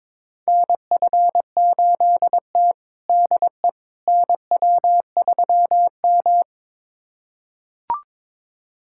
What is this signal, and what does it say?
Morse code, heard as a keyed tone, about 21 wpm, 700 Hz. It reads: NF8T DE NW3M
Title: USA Callsign DE Callsign - 4 Characters - 22wpm